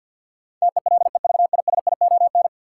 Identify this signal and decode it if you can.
Morse code, heard as a keyed tone, about 51 wpm, 700 Hz. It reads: TELEVISION
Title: Sentences from Top 1000 Words - Review Part 4 - New Word Drill - 50wpm